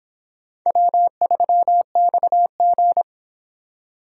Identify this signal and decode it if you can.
Morse code, heard as a keyed tone, about 24 wpm, 700 Hz. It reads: W3XG